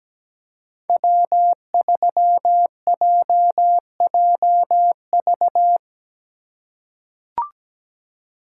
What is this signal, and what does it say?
Morse code, heard as a keyed tone, about 17 wpm, 700 Hz. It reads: W3JJV
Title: USA Call Signs - 5 Characters - 4x Character Spacing - 17wpm